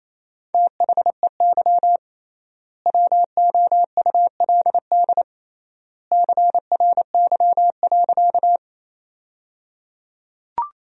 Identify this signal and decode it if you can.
Morse code, heard as a keyed tone, about 28 wpm, 700 Hz. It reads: THEY WOULD CRY.
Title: 3x Word-Spacing - Sentences from Top 500 Words - 28wpm